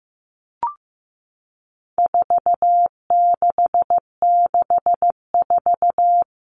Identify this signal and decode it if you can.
Morse code, heard as a keyed tone, about 15 wpm, 700 Hz. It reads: 4664
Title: Numbers - 4-Digit 15wpm